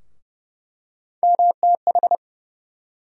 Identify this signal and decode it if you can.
Morse code, heard as a keyed tone, about 30 wpm, 700 Hz. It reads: MTH
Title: USA Call Sign Postfixes - 3 Letters 30wpm